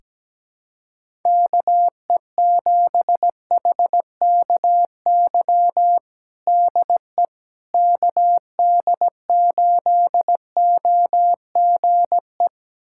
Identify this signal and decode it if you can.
Morse code, heard as a keyed tone, about 17 wpm, 700 Hz. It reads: KE7HKY DE KD8OGE
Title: USA Callsign DE Callsign - 6 Characters - 17wpm